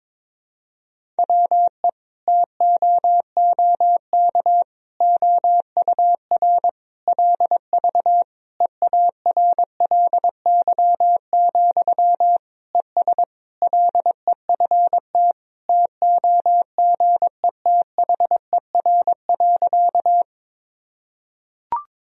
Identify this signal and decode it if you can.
Morse code, heard as a keyed tone, about 23 wpm, 700 Hz. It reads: WE TOOK OUR LV EARLY, ES LEFT TOGETHER.
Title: Sentences from Top 400 Words and Abbreviations 22wpm